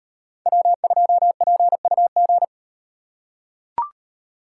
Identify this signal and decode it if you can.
Morse code, heard as a keyed tone, about 38 wpm, 700 Hz. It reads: W2PUG